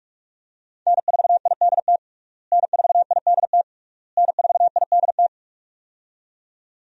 Morse code, heard as a keyed tone, about 42 wpm, 700 Hz. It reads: N4IDT N4IDT N4IDT
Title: USA Call Signs - 5 Characters - Repeated 3x - 45wpm